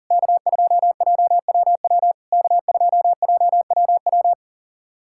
Morse code, heard as a keyed tone, about 40 wpm, 690 Hz. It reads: K2JWW K2JWW
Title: SOTA QSO Practice 40wpm